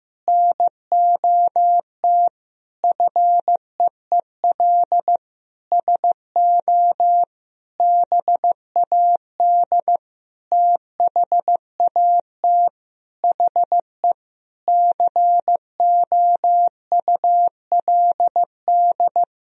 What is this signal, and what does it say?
Morse code, heard as a keyed tone, about 15 wpm, 700 Hz. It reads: NOT FEEL SO BAD THAT HE COULD
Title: Sentences from Top 1000 Words - Review Part 4 - 15wpm